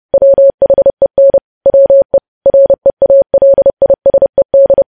Morse code, heard as a keyed tone, about 30 wpm, 550 Hz. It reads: WHEN WE REALISED
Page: news headlines in morse code at 30 wpm